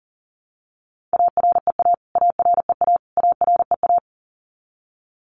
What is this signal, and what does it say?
Morse code, heard as a keyed tone, about 40 wpm, 700 Hz. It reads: AREA AREA AREA
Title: Top 500 Words Repeated 40wpm